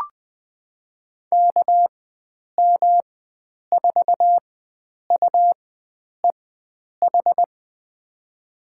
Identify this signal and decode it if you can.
Morse code, heard as a keyed tone, about 20 wpm, spaced as for 9 wpm, 700 Hz. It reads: KM4UEH
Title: USA Call Signs - 4x Character Spacing - 20wpm